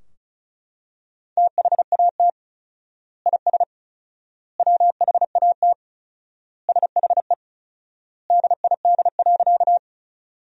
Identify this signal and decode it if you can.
Morse code, heard as a keyed tone, about 36 wpm, 700 Hz. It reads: THAT IS WHAT SHE DID.
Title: 4x Word-Spacing - Sentences from Top 100 Words - 35wpm